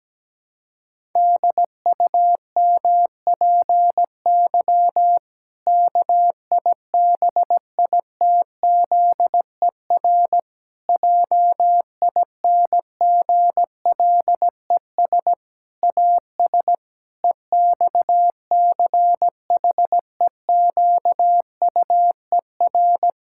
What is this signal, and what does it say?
Morse code, heard as a keyed tone, about 17 wpm, 700 Hz. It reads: DUMPY KIBITZER JINGLES AS EXCHEQUER
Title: Pangram Challenge v2 - 17wpm